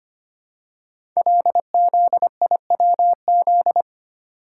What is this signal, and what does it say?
Morse code, heard as a keyed tone, about 25 wpm, 700 Hz. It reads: LZIWZ
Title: Code Groups - 5 Letters - 25wpm